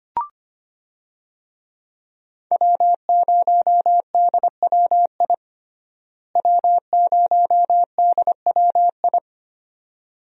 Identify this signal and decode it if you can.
Morse code, heard as a keyed tone, about 25 wpm, 700 Hz. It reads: W0DWI W0DWI
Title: USA Call Signs - 5 Characters - Repeated 2x - 25wpm